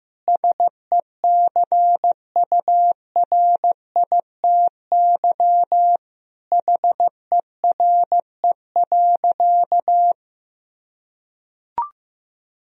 Morse code, heard as a keyed tone, about 16 wpm, 700 Hz. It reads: SECURITY HERE.
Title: Sentences from Top 1000 Words - Review Part 4 - 15wpm